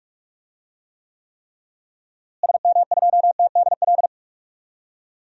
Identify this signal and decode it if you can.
Morse code, heard as a keyed tone, about 42 wpm, 700 Hz. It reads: SM2TDL
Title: DXCC-100 Call Signs - 6 Characters - Repeated 3x - 45wpm